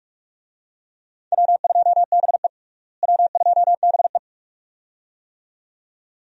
Morse code, heard as a keyed tone, about 46 wpm, 700 Hz. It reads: W2BE W2BE
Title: USA Call Signs - 4 Characters - Repeated 2x - 45wpm